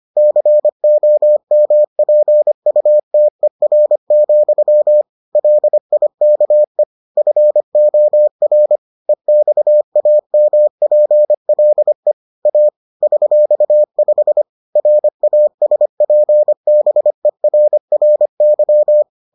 Morse code, heard as a keyed tone, about 25 wpm, 600 Hz. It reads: COMPUTER, LIKE FOR EXAMPLE A $5 RASPBERRY